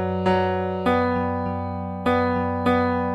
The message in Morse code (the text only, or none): none